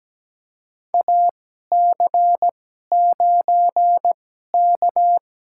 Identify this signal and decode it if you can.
Morse code, heard as a keyed tone, about 17 wpm, spaced as for 12 wpm, 700 Hz. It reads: AC9K
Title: USA Call Signs - 4 Characters - 2x Character Spacing - 17wpm